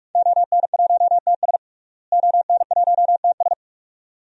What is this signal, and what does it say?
Morse code, heard as a keyed tone, about 46 wpm, 700 Hz. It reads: ON1TS ON1TS
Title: DXCC-100 Call Signs - 5 Characters - Repeated 2x - 45wpm